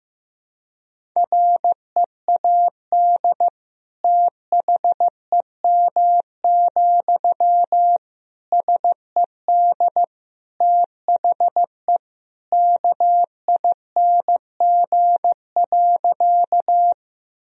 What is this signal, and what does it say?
Morse code, heard as a keyed tone, about 15 wpm, 700 Hz. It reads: READ THEM, SED THE KING.
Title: Sentences from Top 400 Words and Abbreviations 15wpm